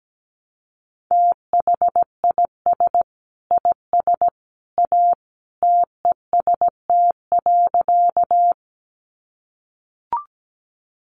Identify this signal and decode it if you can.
Morse code, heard as a keyed tone, about 17 wpm, 700 Hz. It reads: THIS IS A TEST.
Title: Sentences from Top 500 Words 17wpm